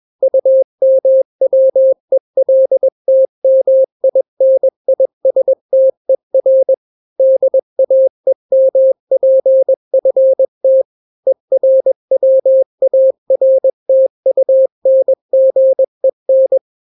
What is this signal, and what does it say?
Morse code, heard as a keyed tone, about 21 wpm, 530 Hz. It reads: UMWELTMINISTER DAEMPFT ERWARTUNGEN
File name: daily / 20wpm